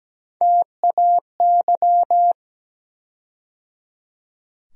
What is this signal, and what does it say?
Morse code, heard as a keyed tone, about 17 wpm, 700 Hz. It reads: TAY
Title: USA Call Sign Postfixes - 3 Letters 17wpm